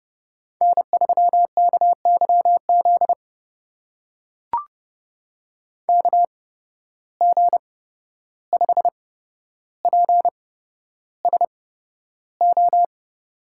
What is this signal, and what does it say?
Morse code, heard as a keyed tone, about 30 wpm, spaced as for 12 wpm, 700 Hz. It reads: N3KYZ KG5PSO
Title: USA Call Signs - 8x Character Spacing - 30wpm